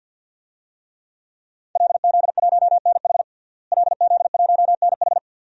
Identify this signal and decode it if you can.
Morse code, heard as a keyed tone, about 51 wpm, 700 Hz. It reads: LZ1NH LZ1NH
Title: DXCC-100 Call Signs - 5 Characters - Repeated 2x - 50wpm